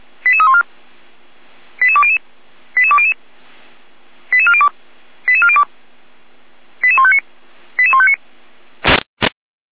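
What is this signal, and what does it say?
Five-tone selective call (ZVEI), heard as quick runs of five tones, tones of about 70 ms. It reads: 88314 88200 88200 88402 88402 88159 88159
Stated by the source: ZVEI five-tone call 88156 received by SDR.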